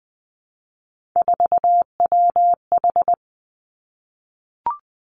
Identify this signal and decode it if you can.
Morse code, heard as a keyed tone, about 20 wpm, 700 Hz. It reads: 4WH
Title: Three Challenging Letter-Number Combinations 20wpm